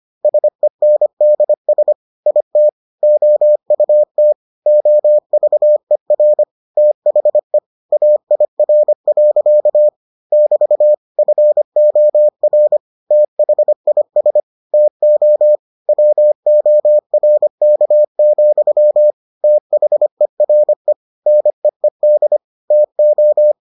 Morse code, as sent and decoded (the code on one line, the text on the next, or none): ... -. -.. ... / .. - / --- ..- - / --- ...- . .-. / - .... / .- .. .-. .-.-.- / -...- / ..-. --- .-. / - .... .. ... / - --- / .-- --- .-. -.- --..-- / - .... .-. / -. . -.. / - ---
SENDS IT OUT OVER THE AIR. <BT> FOR THIS TO WORK, THERE NEED TO